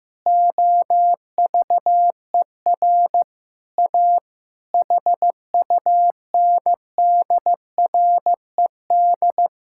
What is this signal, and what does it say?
Morse code, heard as a keyed tone, about 15 wpm, 700 Hz. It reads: OVER A HUNDRED
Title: Sentences from Top 1000 Words - Review Part 5 - 15wpm